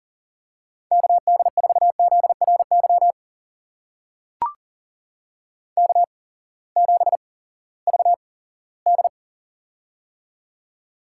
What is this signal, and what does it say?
Morse code, heard as a keyed tone, about 40 wpm, spaced as for 20 wpm, 700 Hz. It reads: KD4ZRY K7VD
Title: USA Call Signs - 8x Character Spacing - 40wpm